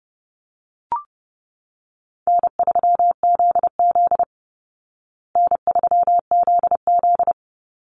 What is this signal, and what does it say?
Morse code, heard as a keyed tone, about 30 wpm, 700 Hz. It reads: N3ZZ N3ZZ
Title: Easy Call Signs Repeated 2x - 30wpm